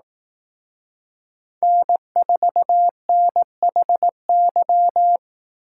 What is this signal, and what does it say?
Morse code, heard as a keyed tone, about 18 wpm, 700 Hz. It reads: N4NHY
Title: Speed Racing - USA 1x3 Call Signs 15wpm